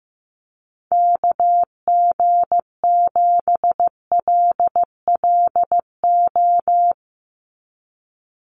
Morse code, heard as a keyed tone, about 15 wpm, 700 Hz. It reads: KG7LLO
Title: USA Call Signs 15wpm